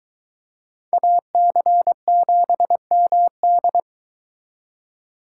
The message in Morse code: AC7MD

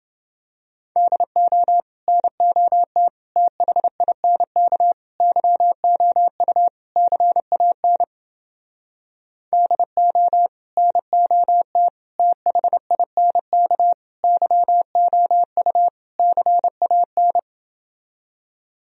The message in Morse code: DO NOT THINK YOU CAN DO NOT THINK YOU CAN

